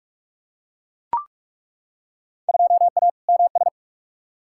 Morse code, heard as a keyed tone, about 42 wpm, 700 Hz. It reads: 2A MS
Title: Field Day Exchanges - 45wpm